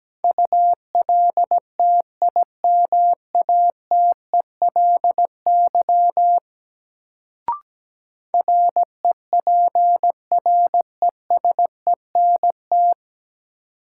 Morse code, heard as a keyed tone, about 17 wpm, 700 Hz. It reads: ULTIMATELY REPRESENT
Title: Sentences from Top 1200 Words - New Word Drill Alt - 17wpm